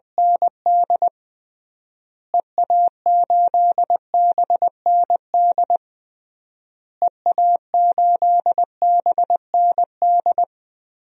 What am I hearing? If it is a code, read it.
Morse code, heard as a keyed tone, about 21 wpm, 700 Hz. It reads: ND EA8BND EA8BND
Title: DXCC-100 Call Signs - 6 Characters - Repeated 3x - 20wpm